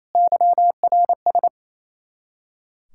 Morse code, heard as a keyed tone, about 34 wpm, 700 Hz. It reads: YRS
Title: Speed-Racing Common QSO Phrases - 28wpm